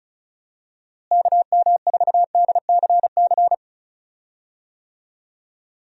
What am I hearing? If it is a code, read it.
Morse code, heard as a keyed tone, about 36 wpm, 700 Hz. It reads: KM4DCC
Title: USA Call Signs - 6 Characters - Repeated 3x - 35wpm